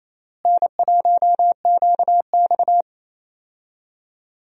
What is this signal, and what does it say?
Morse code, heard as a keyed tone, about 28 wpm, 700 Hz. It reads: N1QX
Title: USA Call Signs - 4 Characters - Repeated 3x - 28wpm